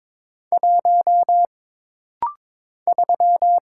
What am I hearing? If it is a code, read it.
Morse code, heard as a keyed tone, about 22 wpm, 700 Hz. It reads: 1 3
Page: This rapid-fire timing is geared to push your ICR (Instant Character Recognition) skills.